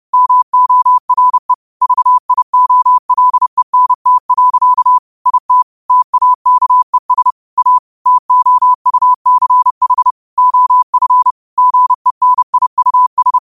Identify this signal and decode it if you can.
Morse code, heard as a keyed tone, about 30 wpm, 1000 Hz. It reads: MORE VIOLENT. IT TAKES A TOUCH OF GENIUS
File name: qotdfiles / 30wpm